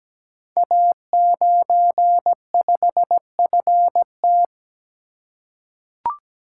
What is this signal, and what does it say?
Morse code, heard as a keyed tone, about 17 wpm, 700 Hz. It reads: A95FT
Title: Code Groups - 5 Letter:Numbers - 17wpm